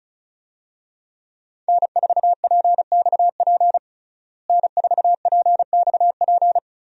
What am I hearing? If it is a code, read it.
Morse code, heard as a keyed tone, about 34 wpm, 700 Hz. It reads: N4PXP N4PXP
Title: USA Call Signs - 5 Characters - Repeated 3x - 35wpm